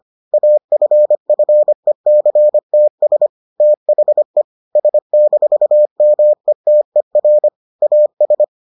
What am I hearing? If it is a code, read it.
Morse code, heard as a keyed tone, about 25 wpm, 600 Hz. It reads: AFFECTS THE S-METER AS